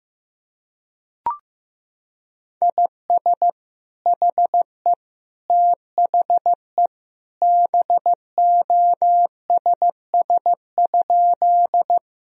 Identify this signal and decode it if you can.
Morse code, heard as a keyed tone, about 15 wpm, 700 Hz. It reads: IS HE THE BOSS?
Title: Sentences from Top 1000 Words - Review Part 4 - 15wpm